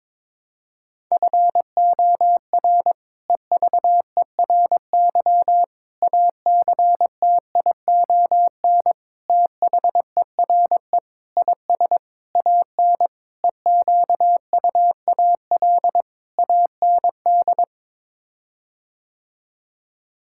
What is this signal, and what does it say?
Morse code, heard as a keyed tone, about 22 wpm, 700 Hz. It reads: FOR EVERY ACTION THERE IS AN EQUAL AND
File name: practice / 22wpm / Sets of 8 Words - 22wpm